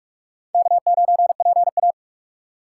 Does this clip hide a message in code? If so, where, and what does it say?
Yes: Morse code from 0.5 to 1.9 s, heard as a keyed tone, about 44 wpm, 700 Hz. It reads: K9PA